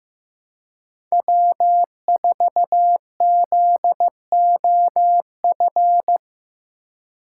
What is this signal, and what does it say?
Morse code, heard as a keyed tone, about 15 wpm, 700 Hz. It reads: W4ZOF
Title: Code Groups - 5 Letter:Numbers - 15wpm